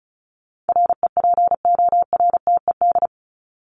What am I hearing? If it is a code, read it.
Morse code, heard as a keyed tone, about 34 wpm, 700 Hz. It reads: REPORTED
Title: ED Suffix - Top 100 Words with Base Words 35wpm